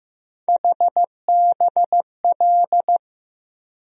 Morse code, heard as a keyed tone, about 15 wpm, 700 Hz. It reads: HBL
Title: USA Call Sign Postfixes - 3 Letters 15wpm